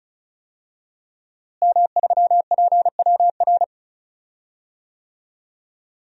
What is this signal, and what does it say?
Morse code, heard as a keyed tone, about 34 wpm, 700 Hz. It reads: M3PWR
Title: DXCC-100 Call Signs - 5 Characters - Repeated 2x - 35wpm